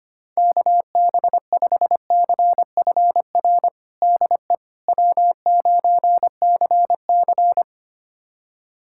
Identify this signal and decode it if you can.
Morse code, heard as a keyed tone, about 25 wpm, 700 Hz. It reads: KB5CFR DE W9CC
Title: USA Callsign DE Callsign 25wpm